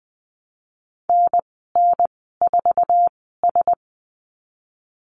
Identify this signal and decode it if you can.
Morse code, heard as a keyed tone, about 20 wpm, spaced as for 14 wpm, 700 Hz. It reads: NN4S
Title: 082 Lesson 14g - Number 2 - Call Signs 20wpm